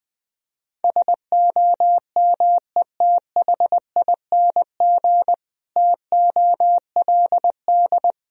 Morse code, heard as a keyed tone, about 20 wpm, 700 Hz. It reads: SOMETHING TOLD
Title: Sentences from Top 1000 Words - Review Part 1 - 20wpm